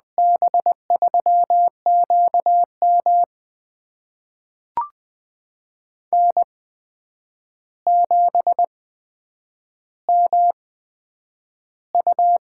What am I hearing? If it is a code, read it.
Morse code, heard as a keyed tone, about 20 wpm, spaced as for 9 wpm, 700 Hz. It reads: B3QM N7MU
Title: USA Call Signs - 8x Character Spacing - 20wpm